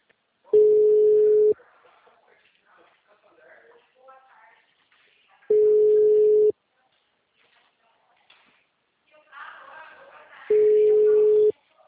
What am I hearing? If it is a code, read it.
Ringback tone (Europe, 425 Hz), with speech.
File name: out- -2016- - - .84286